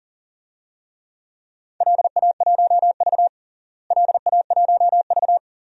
Morse code, heard as a keyed tone, about 40 wpm, 700 Hz. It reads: LA1V LA1V
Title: DXCC-100 Call Signs - 4 Characters - Repeated 2x - 40wpm